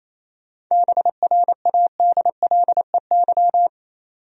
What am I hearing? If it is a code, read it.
Morse code, heard as a keyed tone, about 27 wpm, 700 Hz. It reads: BRADLEY
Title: USA Common Names - 28wpm